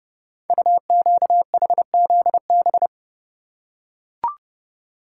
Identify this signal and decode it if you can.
Morse code, heard as a keyed tone, about 30 wpm, 700 Hz. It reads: UQHZB
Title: Code Groups - 5 Letters - 30wpm